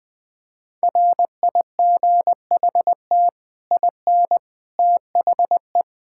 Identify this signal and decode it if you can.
Morse code, heard as a keyed tone, about 21 wpm, 700 Hz. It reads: RIGHT IN THE
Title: Build A Phrase - 5-Word Phrases - 20wpm